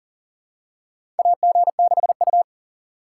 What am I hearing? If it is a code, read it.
Morse code, heard as a keyed tone, about 37 wpm, 700 Hz. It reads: AG6U
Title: USA Call Signs - 4 Characters - 4x Character Spacing - 40wpm